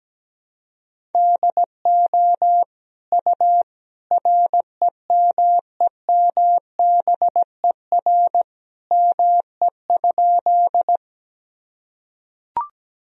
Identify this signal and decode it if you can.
Morse code, heard as a keyed tone, about 17 wpm, 700 Hz. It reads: DO U REMEMBER ME?
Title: Sentences from Top 400 Words and Abbreviations 17wpm